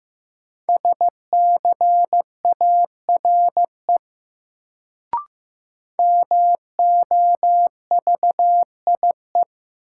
Morse code, heard as a keyed tone, about 15 wpm, 700 Hz. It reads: SCARE MOVIE
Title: Sentences from Top 1000 Words - Review Part 5 - New Word Drill - 15wpm